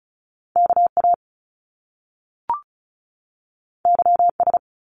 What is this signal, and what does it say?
Morse code, heard as a keyed tone, about 36 wpm, 700 Hz. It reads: KA YS
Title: Top Two Letter Combinations in Words 35wpm